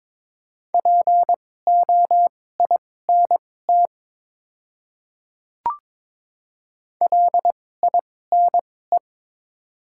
Morse code, heard as a keyed tone, about 22 wpm, spaced as for 16 wpm, 690 Hz. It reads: POINT LINE
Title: 2x Character Spacing - Top 200 Words - 22wpm